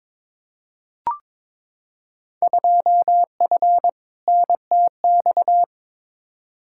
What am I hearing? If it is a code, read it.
Morse code, heard as a keyed tone, about 22 wpm, 700 Hz. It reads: 2F NTX
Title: Field Day Exchanges - 22wpm